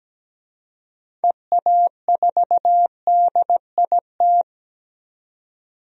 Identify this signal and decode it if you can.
Morse code, heard as a keyed tone, about 17 wpm, 700 Hz. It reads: EA4DIT
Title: DXCC-100 Call Signs - 6 Characters - 17wpm